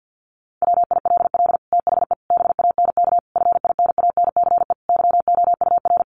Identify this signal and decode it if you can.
Morse code, heard as a keyed tone, about 50 wpm, 700 Hz. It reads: WILL THE BANK FINANCE YOUR
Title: Sentences from Top 1500 Words 50wpm